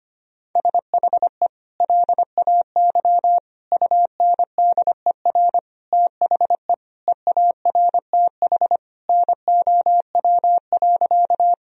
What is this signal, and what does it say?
Morse code, heard as a keyed tone, about 25 wpm, 700 Hz. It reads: SHE LAY UNDER THE EARTH NOW.
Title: Sentences from Top 1000 Words - Review Part 2 - 25wpm